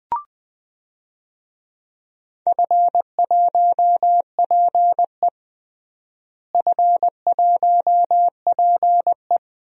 Morse code, heard as a keyed tone, about 20 wpm, 700 Hz. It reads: F1PE F1PE
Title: DXCC-100 Call Signs - 4 Characters - Repeated 2x - 20wpm